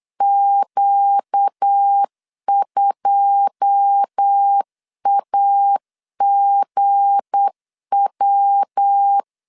Morse code, heard as a keyed tone, about 8 wpm, 800 Hz. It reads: Q2AGW
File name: LEZIONI / 9 2 8wpm 3437